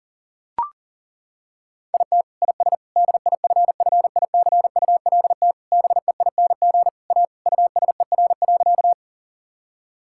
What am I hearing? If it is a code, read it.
Morse code, heard as a keyed tone, about 40 wpm, 700 Hz. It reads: IT IS DIFFICULT BEING A USER.
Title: Sentences from Top 1000 Words - Review Part 5 - 40wpm